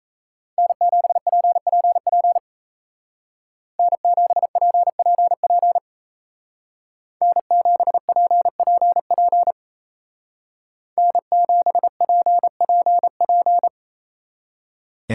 Morse code, heard as a keyed tone, about 33 wpm, 700 Hz. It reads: N7PPP N7PPP N7PPP N7PPP
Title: Speed Racing - USA 1x3 Call Signs 28wpm